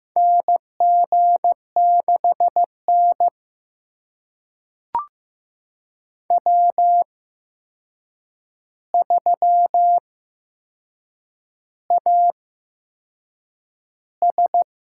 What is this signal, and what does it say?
Morse code, heard as a keyed tone, about 15 wpm, spaced as for 6 wpm, 700 Hz. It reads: NG6N W3AS
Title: USA Call Signs - 4 Characters - 8x Character Spacing - 15wpm